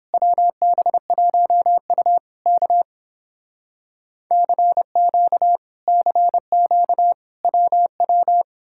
Morse code, heard as a keyed tone, about 30 wpm, 700 Hz. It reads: WB1U K CQ CQ WW